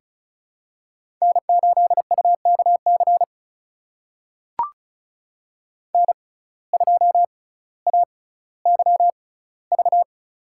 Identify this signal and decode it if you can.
Morse code, heard as a keyed tone, about 35 wpm, spaced as for 18 wpm, 700 Hz. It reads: N8UKC N2AYV
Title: USA Call Signs - 5 Characters - 6x Character Spacing - 35wpm